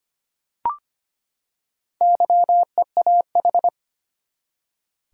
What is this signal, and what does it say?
Morse code, heard as a keyed tone, about 25 wpm, 700 Hz. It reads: YEAH
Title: Four Letter Words - 25wpm